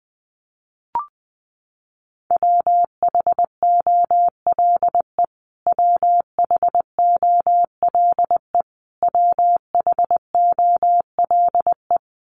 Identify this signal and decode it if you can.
Morse code, heard as a keyed tone, about 20 wpm, 700 Hz. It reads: WHOLE WHOLE WHOLE
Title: Top 500 Words Repeated 20wpm